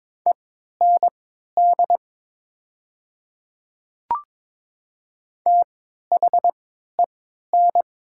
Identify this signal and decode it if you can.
Morse code, heard as a keyed tone, about 22 wpm, spaced as for 12 wpm, 700 Hz. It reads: END THEN
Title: 3x Character Spacing - Top 200 Words - 22wpm